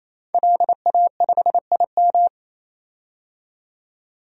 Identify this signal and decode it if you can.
Morse code, heard as a keyed tone, about 28 wpm, 700 Hz. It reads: LA5IM
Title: DXCC-100 Call Signs - 5 Characters - 28wpm